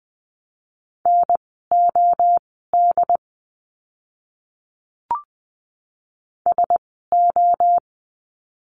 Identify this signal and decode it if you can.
Morse code, heard as a keyed tone, about 20 wpm, spaced as for 14 wpm, 700 Hz. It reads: NOD SO
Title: 068 Lesson 12f - Letter d - Words 20wpm